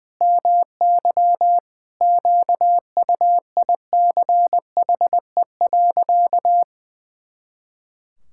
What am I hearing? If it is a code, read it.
Morse code, heard as a keyed tone, about 20 wpm, 700 Hz. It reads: MY QUICHE.
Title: Pangram Challenge v2 - 20wpm